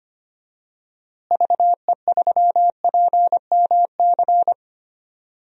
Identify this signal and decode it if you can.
Morse code, heard as a keyed tone, about 25 wpm, 700 Hz. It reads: VE3PMC
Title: DXCC-100 Call Signs - 6 Characters - Repeated 3x - 25wpm